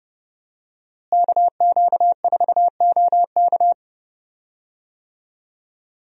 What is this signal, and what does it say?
Morse code, heard as a keyed tone, about 31 wpm, 700 Hz. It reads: KQ4OK
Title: USA Call Signs - 5 Characters - Repeated 3x - 30wpm